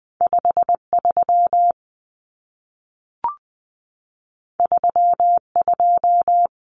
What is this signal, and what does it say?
Morse code, heard as a keyed tone, about 20 wpm, 700 Hz. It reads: 5 3 3 2
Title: Common Cut Numbers - 2-Digit 20wpm